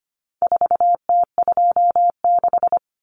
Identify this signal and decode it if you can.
Morse code, heard as a keyed tone, about 25 wpm, 700 Hz. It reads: 4T26
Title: Common Cut Numbers - 4-Digit 25wpm